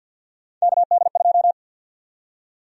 Morse code, heard as a keyed tone, about 53 wpm, 700 Hz. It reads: KD2
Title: USA Call Sign Prefixes - 50wpm